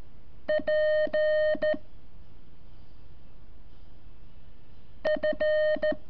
Morse code, heard as a keyed tone, about 12 wpm, 610 Hz. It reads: P F
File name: curso / ejercicio 24